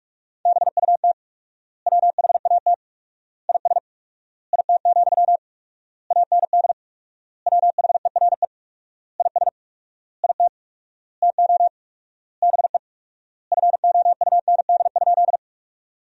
Morse code, heard as a keyed tone, about 46 wpm, 700 Hz. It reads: BUT WHAT IS IT, AND WHERE IS IT TO BE FOUND?
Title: 4x Word-Spacing - Sentences from Top 300 Words - 45wpm